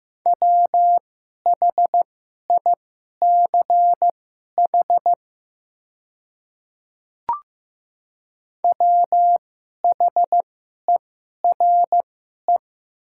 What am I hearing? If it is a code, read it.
Morse code, heard as a keyed tone, about 15 wpm, spaced as for 11 wpm, 700 Hz. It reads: WHICH WHERE
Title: 2x Character Spacing - Top 500 Words - 15wpm